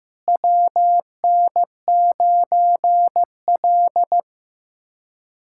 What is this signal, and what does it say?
Morse code, heard as a keyed tone, about 15 wpm, 700 Hz. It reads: WN9L
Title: USA Callsign DE Callsign 15wpm